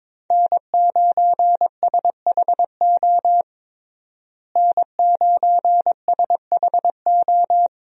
Morse code, heard as a keyed tone, about 22 wpm, 700 Hz. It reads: N9SHO N9SHO
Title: USA Call Signs - 5 Characters - Repeated 2x - 22wpm